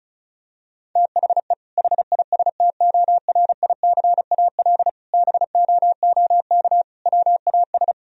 Morse code, heard as a keyed tone, about 34 wpm, 700 Hz. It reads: THE HISTORICAL BOOK WAS
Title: Sentences from Top 1000 Words - Review Part 4 - 35wpm